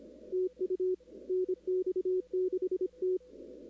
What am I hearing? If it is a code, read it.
Morse code, heard as a keyed tone, about 25 wpm, 370 Hz, with radio noise. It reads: TU NX6T
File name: contests / 2019-cw-wpx-kt5j-k5pi- -2330 0-17